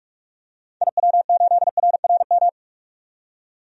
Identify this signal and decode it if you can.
Morse code, heard as a keyed tone, about 46 wpm, 700 Hz. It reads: IW8RRM
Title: DXCC-100 Call Signs - 6 Characters - 45wpm